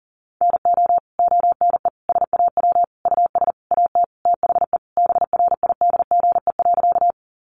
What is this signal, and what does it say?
Morse code, heard as a keyed tone, about 40 wpm, 700 Hz. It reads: NO ONE SAW US AT THE BRIDGE.
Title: Sentences from Top 1200 Words 40wpm